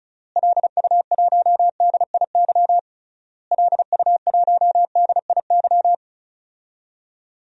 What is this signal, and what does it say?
Morse code, heard as a keyed tone, about 34 wpm, 700 Hz. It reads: LU1DIY LU1DIY
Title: DXCC-100 Call Signs - 6 Characters - Repeated 3x - 35wpm